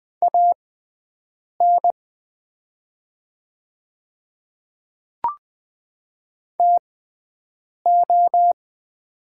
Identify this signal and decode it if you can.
Morse code, heard as a keyed tone, about 20 wpm, spaced as for 7 wpm, 700 Hz. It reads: AN TO
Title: 6x Character Spacing - Top 100 Words - 20wpm